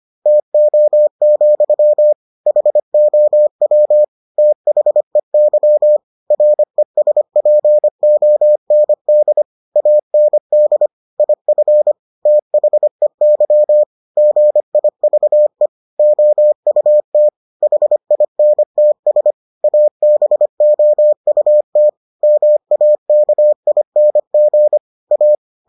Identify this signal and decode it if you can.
Morse code, heard as a keyed tone, about 25 wpm, 600 Hz. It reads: TO, HOW THEY RESPOND AND IF THEY GIVE OUT HINTS ABOUT MAKING A